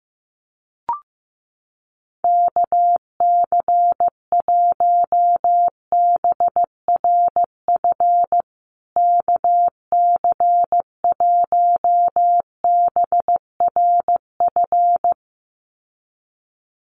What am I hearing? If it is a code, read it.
Morse code, heard as a keyed tone, about 15 wpm, 700 Hz. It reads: KC1BRF KC1BRF
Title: USA Call Signs Repeated 2x 15wpm